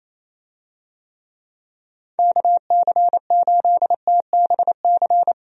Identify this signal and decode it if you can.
Morse code, heard as a keyed tone, about 28 wpm, 700 Hz. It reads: KC8TBC
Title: USA Call Signs - 6 Characters - Repeated 2x - 28wpm